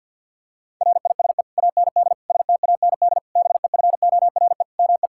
Morse code, heard as a keyed tone, about 50 wpm, 700 Hz. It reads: RISE AND STAND BEFORE ME
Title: Sentences from Top 1600 Words 50wpm